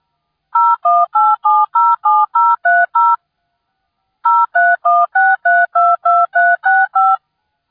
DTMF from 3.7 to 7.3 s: 0316322365